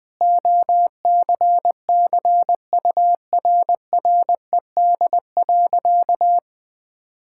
Morse code, heard as a keyed tone, about 20 wpm, 700 Hz. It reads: OCCURRED.
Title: Sentences with 6-Word Phrases - 20wpm